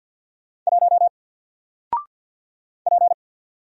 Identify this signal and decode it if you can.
Morse code, heard as a keyed tone, about 48 wpm, 700 Hz. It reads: 1 P
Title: Single Letter-Number Alt 50wpm